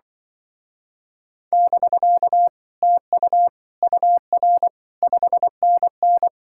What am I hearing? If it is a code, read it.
Morse code, heard as a keyed tone, about 24 wpm, 700 Hz. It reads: <BK> TU UR 5NN
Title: Speed-Racing Common QSO Phrases Encore - 20wpm